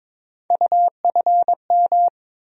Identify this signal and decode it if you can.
Morse code, heard as a keyed tone, about 22 wpm, 700 Hz. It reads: UFM
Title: USA Call Sign Postfixes - 3 Letters 22wpm